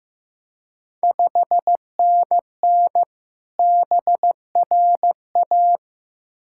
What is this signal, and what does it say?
Morse code, heard as a keyed tone, about 15 wpm, 700 Hz. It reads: 5NN BRA
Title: Ontario QSO Party Exchange 15wpm